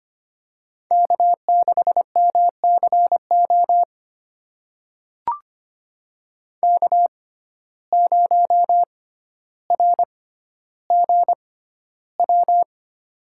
Morse code, heard as a keyed tone, about 25 wpm, spaced as for 13 wpm, 700 Hz. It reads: K6MCO K0RGW
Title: USA Call Signs - 5 Characters - 6x Character Spacing - 25wpm